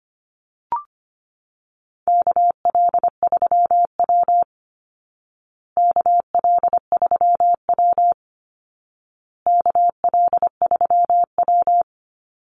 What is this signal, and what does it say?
Morse code, heard as a keyed tone, about 25 wpm, 700 Hz. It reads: KL3W KL3W KL3W
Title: Call Signs Repeated 2x - 25wpm